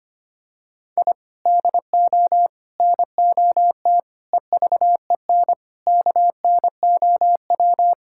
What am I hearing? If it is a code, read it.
Morse code, heard as a keyed tone, about 25 wpm, 700 Hz. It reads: I DO NOT EVEN KNOW
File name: practice / 25wpm / Sentences with 4-Word Phrases - 25wpm